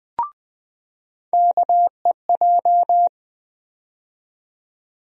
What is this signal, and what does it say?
Morse code, heard as a keyed tone, about 20 wpm, 700 Hz. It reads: KEJ
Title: USA Call Sign Postfixes - 3 Letters 20wpm